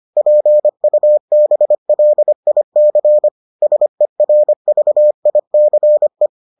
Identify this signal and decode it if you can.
Morse code, heard as a keyed tone, about 25 wpm, 600 Hz. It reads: PUBLIC SERVICE